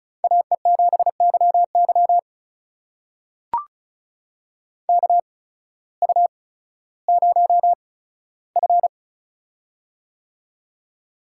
Morse code, heard as a keyed tone, about 36 wpm, spaced as for 16 wpm, 700 Hz. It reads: AE7YY KU0F
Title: USA Call Signs - 8x Character Spacing - 35wpm